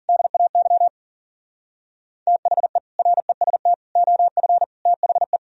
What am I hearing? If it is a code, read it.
Morse code, heard as a keyed tone, about 53 wpm, 700 Hz. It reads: DAY THE REST OF THE